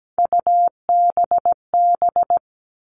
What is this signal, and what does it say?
Morse code, heard as a keyed tone, about 17 wpm, 700 Hz. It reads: UBB
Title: All Cut Numbers - 3-Digit 17wpm